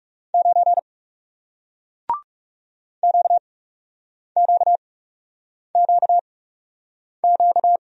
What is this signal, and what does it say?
Morse code, heard as a keyed tone, about 39 wpm, spaced as for 8 wpm, 700 Hz. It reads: 9 QQQQ